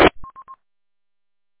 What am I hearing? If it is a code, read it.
There is no code.